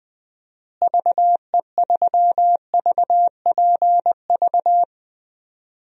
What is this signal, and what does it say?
Morse code, heard as a keyed tone, about 20 wpm, 700 Hz. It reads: VE3VPV
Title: DXCC-100 Call Signs - 6 Characters - Repeated 3x - 20wpm